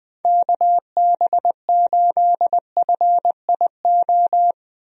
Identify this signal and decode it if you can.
Morse code, heard as a keyed tone, about 20 wpm, 700 Hz. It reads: KB8FIO
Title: USA Call Signs - 6 Characters - 20wpm